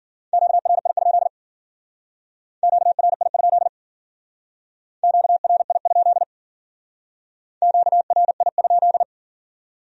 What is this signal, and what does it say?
Morse code, heard as a keyed tone, about 53 wpm, 700 Hz. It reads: QRI? QRI? QRI? QRI?